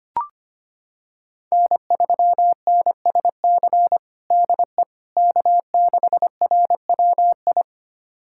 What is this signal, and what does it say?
Morse code, heard as a keyed tone, about 25 wpm, 700 Hz. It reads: N3NSC DE K6RWI
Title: USA Callsign DE Callsign - 5 Characters - 25wpm